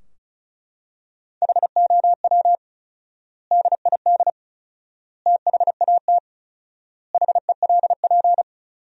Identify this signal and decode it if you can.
Morse code, heard as a keyed tone, about 34 wpm, 700 Hz. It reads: HOW DID THAT HELP
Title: 4x Word-Spacing - Sentences from Top 200 Words - 35wpm